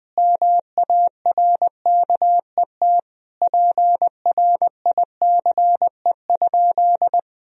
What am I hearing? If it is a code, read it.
Morse code, heard as a keyed tone, about 20 wpm, 700 Hz. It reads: MARKET PRICE?
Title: Sentences from Top 1000 Words - Review Part 3 - 20wpm